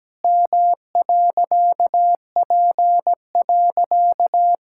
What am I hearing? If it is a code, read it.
Morse code, heard as a keyed tone, about 17 wpm, 700 Hz. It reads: M.P.
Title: Pangram Challenge v2 - 17wpm